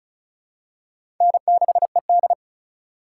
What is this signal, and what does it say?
Morse code, heard as a keyed tone, about 34 wpm, 700 Hz. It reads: N6ED
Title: USA Call Signs - 4 Characters - 3x Character Spacing - 35wpm